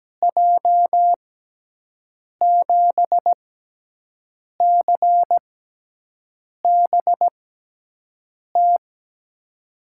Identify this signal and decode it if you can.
Morse code, heard as a keyed tone, about 17 wpm, spaced as for 6 wpm, 700 Hz. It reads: J7CBT
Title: USA Call Signs - 6x Character Spacing - 17wpm